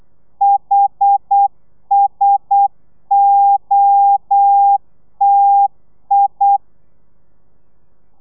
Morse code, heard as a keyed tone, about 8 wpm, 800 Hz. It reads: HSOTI
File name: telegrafia / i2tfj / Lez1 35